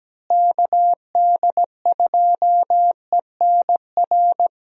Morse code, heard as a keyed tone, about 17 wpm, 700 Hz. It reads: KD2ENR
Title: USA Call Signs - 6 Characters - Repeated 2x - 17wpm